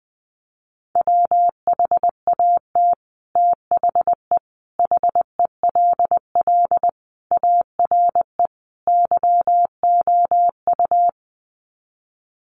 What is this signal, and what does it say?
Morse code, heard as a keyed tone, about 20 wpm, 700 Hz. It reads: WHAT THE HELL ARE YOU
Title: Sets of 5 Words Encore 20wpm